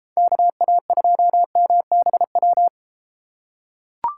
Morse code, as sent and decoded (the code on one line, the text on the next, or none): -.- .- ..--- -- -... .--
KA2MBW